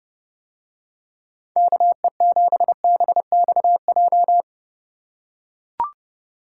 Morse code, heard as a keyed tone, about 29 wpm, 700 Hz. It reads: KE7BXJ